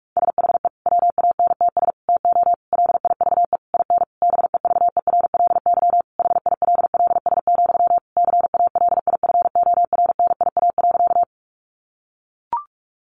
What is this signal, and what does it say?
Morse code, heard as a keyed tone, about 44 wpm, 700 Hz. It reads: SHE WANTS TO LIVE IN BEVERLY HILLS, CALIFORNIA.
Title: Easy Phrases 45wpm